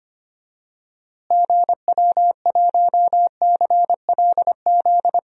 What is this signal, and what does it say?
Morse code, heard as a keyed tone, about 25 wpm, 700 Hz. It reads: GW1CLZ
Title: DXCC-100 Call Signs - 6 Characters - Repeated 3x - 25wpm